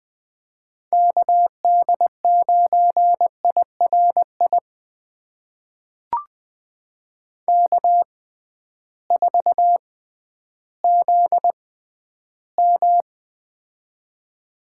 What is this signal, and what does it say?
Morse code, heard as a keyed tone, about 20 wpm, spaced as for 12 wpm, 700 Hz. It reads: KD9IRI K4ZM
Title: USA Call Signs - 6x Character Spacing - 20wpm